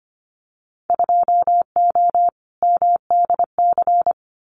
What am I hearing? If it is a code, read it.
Morse code, heard as a keyed tone, about 25 wpm, 700 Hz. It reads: 2O MDC
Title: Winter Field Day Exchanges Encore 25wpm